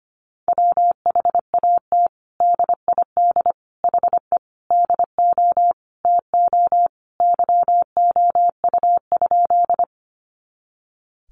Morse code, heard as a keyed tone, about 25 wpm, 700 Hz. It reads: WHAT DID HE DO TO YOU?
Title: Sentences from Top 100 Words 25wpm